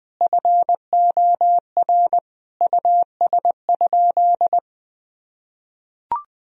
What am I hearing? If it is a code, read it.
Morse code, heard as a keyed tone, about 20 wpm, 700 Hz. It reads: FOR US?
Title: Sentences from Top 1000 Words - Review Part 3 - 20wpm